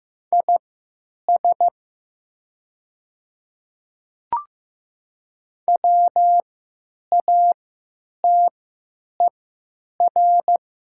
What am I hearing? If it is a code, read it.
Morse code, heard as a keyed tone, about 15 wpm, spaced as for 9 wpm, 690 Hz. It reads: IS WATER
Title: 3x Character Spacing - Top 100 Words - 15wpm